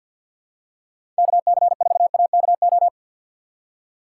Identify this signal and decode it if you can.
Morse code, heard as a keyed tone, about 47 wpm, 700 Hz. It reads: KC4AKO
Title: USA Call Signs - 6x Character Spacing - 50wpm